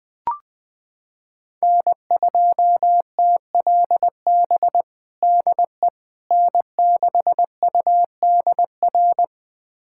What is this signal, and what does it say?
Morse code, heard as a keyed tone, about 20 wpm, 700 Hz. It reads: N2TLB DE N6UDR
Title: USA Callsign DE Callsign - 5 Characters - 20wpm